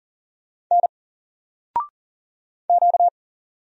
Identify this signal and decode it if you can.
Morse code, heard as a keyed tone, about 39 wpm, 700 Hz. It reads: N Q
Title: Single Character (Letter:Number:Punctuation:Prosign) 40wpm